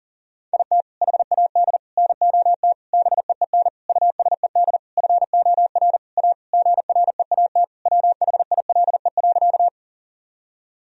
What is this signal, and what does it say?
Morse code, heard as a keyed tone, about 40 wpm, 700 Hz. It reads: IT HAD NOT BEEN USED FOR A GREAT WHILE.
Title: Sentences from Top 1000 Words - Review Part 1 - 40wpm